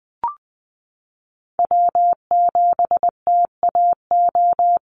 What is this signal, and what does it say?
Morse code, heard as a keyed tone, about 20 wpm, 700 Hz. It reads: W7TAO
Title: NA SOTA Chaser Callsigns 20wpm